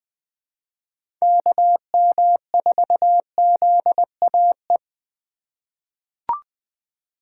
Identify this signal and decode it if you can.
Morse code, heard as a keyed tone, about 20 wpm, 700 Hz. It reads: KM4ZAE
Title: USA Call Signs - 3x Character Spacing - 20wpm